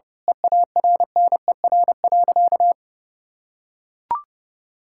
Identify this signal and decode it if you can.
Morse code, heard as a keyed tone, about 32 wpm, 700 Hz. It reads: EARNER.
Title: Pangram Challenge v2 - 30wpm